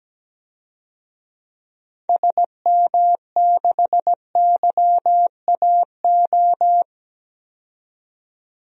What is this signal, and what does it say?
Morse code, heard as a keyed tone, about 17 wpm, 700 Hz. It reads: SM6YAO
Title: DXCC-100 Call Signs - 6 Characters - 17wpm